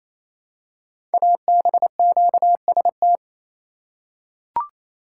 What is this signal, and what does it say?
Morse code, heard as a keyed tone, about 27 wpm, 700 Hz. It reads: ABQST
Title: Code Groups - 5 Letters - 28wpm